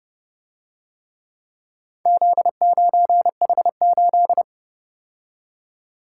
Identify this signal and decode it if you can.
Morse code, heard as a keyed tone, about 32 wpm, 700 Hz. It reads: Z9H8